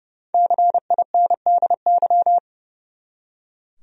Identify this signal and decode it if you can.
Morse code, heard as a keyed tone, about 30 wpm, 700 Hz. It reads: CINDY
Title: USA Common Names - Max 5 Letters - 30wpm